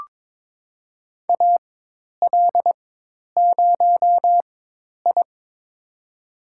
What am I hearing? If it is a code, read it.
Morse code, heard as a keyed tone, about 22 wpm, spaced as for 10 wpm, 700 Hz. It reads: AL0I
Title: USA Call Signs - 4x Character Spacing - 22wpm